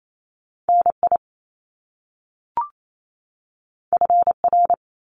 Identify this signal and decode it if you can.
Morse code, heard as a keyed tone, about 28 wpm, 700 Hz. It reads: NI FR
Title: Top Two Letter Combinations in Words 28wpm